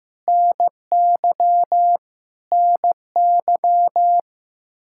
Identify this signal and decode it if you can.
Morse code, heard as a keyed tone, about 15 wpm, 700 Hz. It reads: NY NY
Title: Practice QSO - Exchange 1 - Beginning - 15wpm